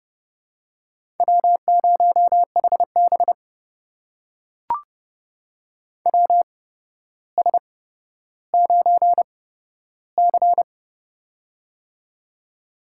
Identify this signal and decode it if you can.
Morse code, heard as a keyed tone, about 31 wpm, spaced as for 13 wpm, 700 Hz. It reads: W0HB WS9C
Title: USA Call Signs - 8x Character Spacing - 30wpm